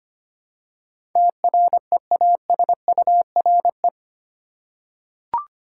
Morse code, heard as a keyed tone, about 25 wpm, 700 Hz. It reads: TREASURE
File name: 8 Letter Words - First 3 Letters Given - 25wpm